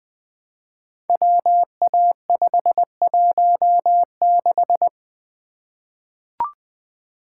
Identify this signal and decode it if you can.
Morse code, heard as a keyed tone, about 20 wpm, 700 Hz. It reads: WA516
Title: Code Groups - 5 Letter:Numbers - 20wpm